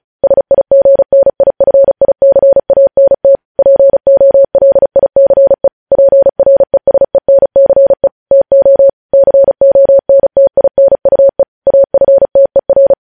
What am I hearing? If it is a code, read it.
Morse code, heard as a keyed tone, about 36 wpm, 550 Hz. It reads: SIGNIFICANT POLICE PRESENCE TO CONTINUE AFTER
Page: news headlines in morse code at 35 wpm